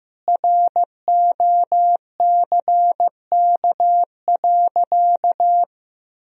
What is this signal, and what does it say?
Morse code, heard as a keyed tone, about 15 wpm, 700 Hz. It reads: ROCK.
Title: Sentences from Top 1000 Words - Review Part 1 - 15wpm